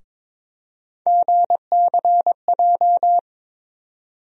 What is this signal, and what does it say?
Morse code, heard as a keyed tone, about 22 wpm, 700 Hz. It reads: GCJ
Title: USA Call Sign Postfixes - 3 Letters 22wpm